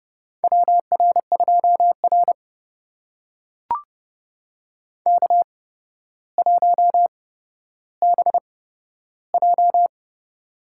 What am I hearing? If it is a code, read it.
Morse code, heard as a keyed tone, about 30 wpm, spaced as for 13 wpm, 700 Hz. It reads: WR2R K1BJ
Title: USA Call Signs - 4 Characters - 8x Character Spacing - 30wpm